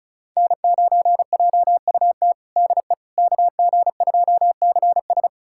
Morse code, heard as a keyed tone, about 36 wpm, 700 Hz. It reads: N9JUT DE KG2CS
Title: USA Callsign DE Callsign - 5 Characters - 35wpm